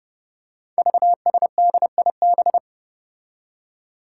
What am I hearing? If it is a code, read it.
Morse code, heard as a keyed tone, about 30 wpm, 700 Hz. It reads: VSDIB
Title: Five Challenging Letter-Number Combinations 30wpm